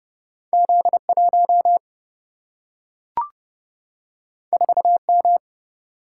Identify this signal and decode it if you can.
Morse code, heard as a keyed tone, about 30 wpm, 700 Hz. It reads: Z1 4M
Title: Single-Letter Single-Number 30wpm